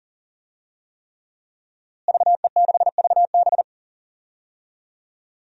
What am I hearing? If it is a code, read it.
Morse code, heard as a keyed tone, about 40 wpm, 700 Hz. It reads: VE6VB
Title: DXCC-100 Call Signs - 5 Characters - Repeated 3x - 40wpm